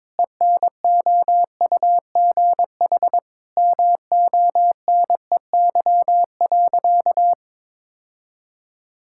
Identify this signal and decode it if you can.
Morse code, heard as a keyed tone, about 22 wpm, 700 Hz. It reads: ENOUGH MONEY.
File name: practice / 22wpm / Sentences with 4-Word Phrases - 22wpm